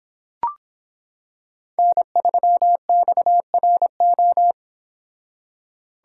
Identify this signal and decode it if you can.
Morse code, heard as a keyed tone, about 25 wpm, 700 Hz. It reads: N3XRO